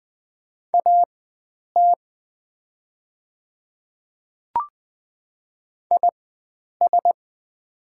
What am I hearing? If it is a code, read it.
Morse code, heard as a keyed tone, about 20 wpm, spaced as for 9 wpm, 700 Hz. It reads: AT IS
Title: 4x Character Spacing - Top 100 Words - 20wpm